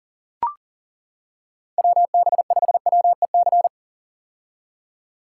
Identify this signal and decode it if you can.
Morse code, heard as a keyed tone, about 40 wpm, 700 Hz. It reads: WB5WEC
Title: K1USN SST QSO Practice - 40wpm